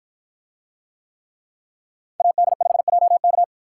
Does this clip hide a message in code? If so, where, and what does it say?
Yes: Morse code from 2.2 to 3.4 s, heard as a keyed tone, about 50 wpm, 700 Hz. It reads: AD5JK